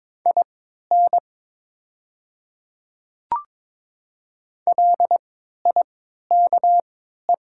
Morse code, heard as a keyed tone, about 22 wpm, spaced as for 12 wpm, 700 Hz. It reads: IN LIKE
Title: 3x Character Spacing - Top 100 Words - 22wpm